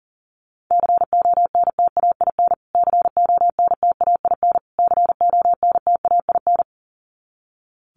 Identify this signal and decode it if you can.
Morse code, heard as a keyed tone, about 40 wpm, 700 Hz. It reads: CONTAIN CONTAIN CONTAIN
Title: Top 500 Words Repeated 40wpm